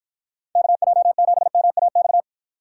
Morse code, heard as a keyed tone, about 54 wpm, 700 Hz. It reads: KJ7MAX